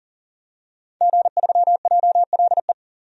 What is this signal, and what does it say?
Morse code, heard as a keyed tone, about 40 wpm, 700 Hz. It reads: G3JLE
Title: DXCC-100 Call Signs - 5 Characters - 40wpm